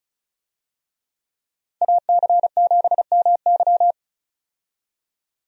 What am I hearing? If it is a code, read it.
Morse code, heard as a keyed tone, about 36 wpm, 700 Hz. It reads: AC7MY
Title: USA Call Signs - 5 Characters - 35wpm